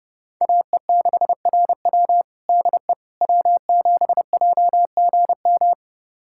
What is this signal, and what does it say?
Morse code, heard as a keyed tone, about 30 wpm, 700 Hz. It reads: AE6RW DE W7JGM
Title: USA Callsign DE Callsign - 5 Characters - 30wpm